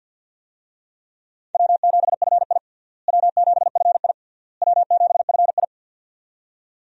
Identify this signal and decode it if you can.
Morse code, heard as a keyed tone, about 48 wpm, 700 Hz. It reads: W7FI W7FI W7FI
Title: USA Call Signs - 4 Characters - Repeated 3x - 50wpm